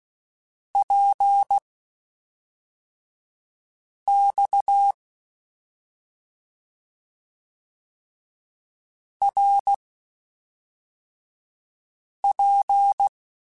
Morse code, heard as a keyed tone, about 16 wpm, spaced as for 3 wpm, 800 Hz. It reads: PX RP